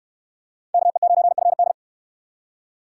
Morse code, heard as a keyed tone, about 61 wpm, 700 Hz. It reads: KE9FD